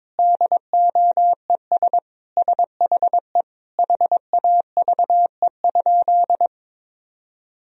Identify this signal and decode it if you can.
Morse code, heard as a keyed tone, about 22 wpm, 700 Hz. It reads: DOES SHE HAVE?
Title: Sentences from Top 1000 Words - Review Part 1 - 22wpm